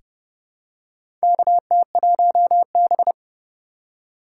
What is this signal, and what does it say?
Morse code, heard as a keyed tone, about 29 wpm, 700 Hz. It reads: KT1B
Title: K1USN SST QSO Practice - 30wpm